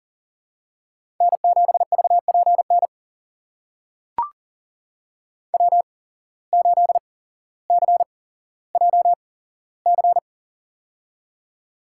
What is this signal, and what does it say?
Morse code, heard as a keyed tone, about 40 wpm, spaced as for 17 wpm, 700 Hz. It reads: N7VPN W8CJC
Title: USA Call Signs - 5 Characters - 8x Character Spacing - 40wpm